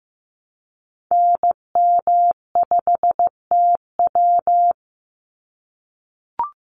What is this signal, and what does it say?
Morse code, heard as a keyed tone, about 15 wpm, 700 Hz. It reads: NM5TW
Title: NA SOTA Activator Callsigns 15wpm